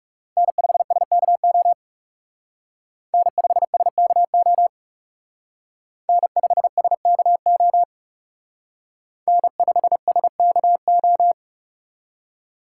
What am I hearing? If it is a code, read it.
Morse code, heard as a keyed tone, about 36 wpm, 700 Hz. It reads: N5SKO N5SKO N5SKO N5SKO
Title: Speed Racing - USA 1x3 Call Signs 30wpm